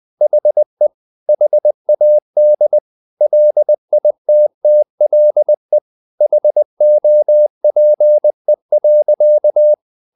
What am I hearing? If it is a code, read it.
Morse code, heard as a keyed tone, about 20 wpm, 600 Hz. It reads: HE HAD LITTLE HOPE.
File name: the old man and the sea part 2 20wpm